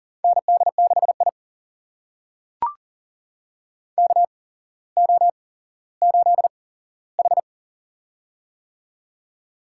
Morse code, heard as a keyed tone, about 40 wpm, spaced as for 17 wpm, 700 Hz. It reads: ND6I KO8H
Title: USA Call Signs - 4 Characters - 8x Character Spacing - 40wpm